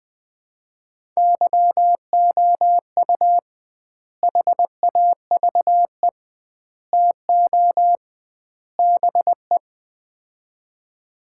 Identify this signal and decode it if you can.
Morse code, heard as a keyed tone, about 20 wpm, 700 Hz. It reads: YOU HAVE TO BE
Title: 2x Word-Spacing - Sets of 4 Words - 20wpm